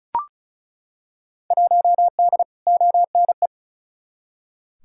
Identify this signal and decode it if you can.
Morse code, heard as a keyed tone, about 34 wpm, 700 Hz. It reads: 1D ONE
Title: Field Day Exchanges - 35wpm